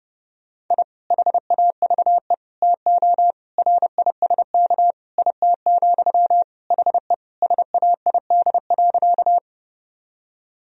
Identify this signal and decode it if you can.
Morse code, heard as a keyed tone, about 28 wpm, 700 Hz. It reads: I HAVE TO RISK IT, HE SAID.
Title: Sentences from Top 1000 Words - Review Part 4 - 30wpm